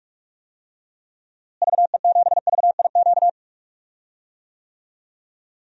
Morse code, heard as a keyed tone, about 46 wpm, 700 Hz. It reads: VE7VIQ
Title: DXCC-100 Call Signs - 6 Characters - Repeated 2x - 45wpm